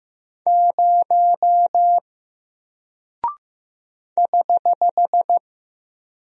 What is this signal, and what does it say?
Morse code, heard as a keyed tone, about 15 wpm, 700 Hz. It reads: 0 <HH>
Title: Single Character (Letter:Number:Punctuation:Prosign) Alt 15wpm